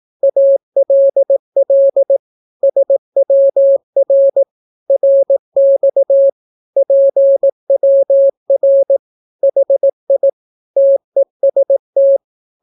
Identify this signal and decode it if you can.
Morse code, heard as a keyed tone, about 18 wpm, 550 Hz. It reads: ALL SWR RX PWR HI TEST
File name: naukapodstaw / skroty / skroty